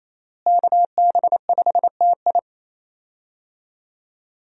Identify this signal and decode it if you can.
Morse code, heard as a keyed tone, about 28 wpm, 700 Hz. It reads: KB5TI
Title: USA Call Signs - 5 Characters - Repeated 2x - 28wpm